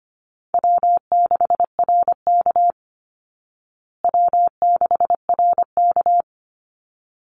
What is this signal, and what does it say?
Morse code, heard as a keyed tone, about 25 wpm, 700 Hz. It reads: W6RK W6RK
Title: Easy Call Signs Repeated 2x - 25wpm